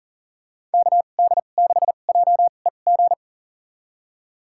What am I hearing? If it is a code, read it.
Morse code, heard as a keyed tone, about 40 wpm, spaced as for 29 wpm, 700 Hz. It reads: KD6JEG
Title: USA Call Signs - 6 Characters - 2x Character Spacing - 40wpm